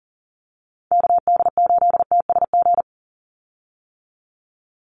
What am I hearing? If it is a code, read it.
Morse code, heard as a keyed tone, about 40 wpm, 700 Hz. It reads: KD8TSG
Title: USA Call Signs 40wpm